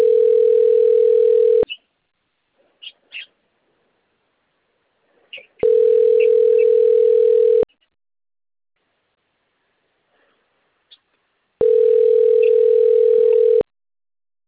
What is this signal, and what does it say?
Ringback tone (North America), 440 + 480 Hz.